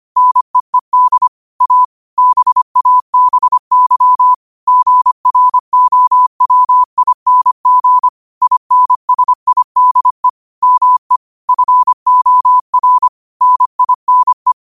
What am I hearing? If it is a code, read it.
Morse code, heard as a keyed tone, about 25 wpm, 1000 Hz. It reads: NEED A BABY GROWING INSIDE ME FOR NINE
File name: qotdfiles / 25wpm